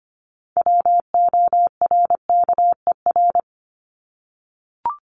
Morse code, heard as a keyed tone, about 25 wpm, 700 Hz. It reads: WORKER
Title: ES:S Suffix - Top 100 Words with Base Words 25wpm